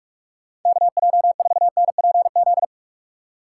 Morse code, heard as a keyed tone, about 44 wpm, 700 Hz. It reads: KJ4NPZ